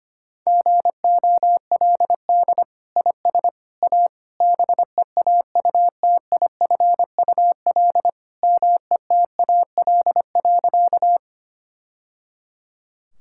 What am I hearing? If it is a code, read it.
Morse code, heard as a keyed tone, about 25 wpm, 700 Hz. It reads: GOLD IS A BEAUTIFUL METAL.
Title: Sentences from Top 1000 Words - Review Part 6 - 25wpm